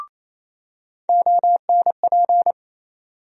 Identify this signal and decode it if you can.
Morse code, heard as a keyed tone, about 28 wpm, 700 Hz. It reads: ONP
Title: USA Call Sign Postfixes - 3 Letters 28wpm